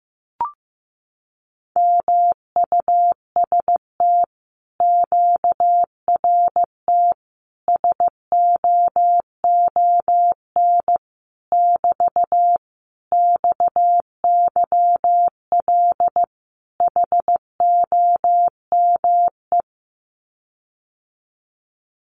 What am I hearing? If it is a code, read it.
Morse code, heard as a keyed tone, about 15 wpm, 700 Hz. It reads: MUST QRT SOON <BT> XYL HOME
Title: Common QSO Phrases Encore 15wpm